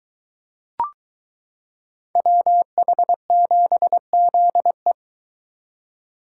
Morse code, heard as a keyed tone, about 23 wpm, 700 Hz. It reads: WH7ZE